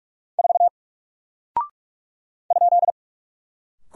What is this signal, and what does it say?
Morse code, heard as a keyed tone, about 44 wpm, 700 Hz. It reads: 4 ?
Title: Single Character (Letter:Number:Punctuation:Prosign) 45wpm